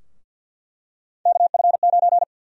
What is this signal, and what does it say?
Morse code, heard as a keyed tone, about 48 wpm, 700 Hz. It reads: KF9
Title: USA Call Sign Prefixes - 50wpm